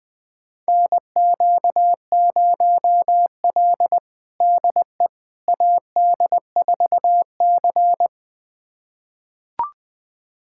Morse code, heard as a keyed tone, about 21 wpm, 700 Hz. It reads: NQ0L DE AD4C
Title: USA Callsign DE Callsign - 4 Characters - 20wpm